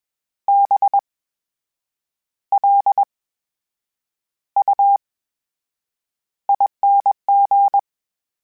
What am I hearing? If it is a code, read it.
Morse code, heard as a keyed tone, about 21 wpm, spaced as for 8 wpm, 800 Hz. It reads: BLUING
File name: Special INGs-mp3s / Special INGs-mp3s / VerySlowToFaster Ing @20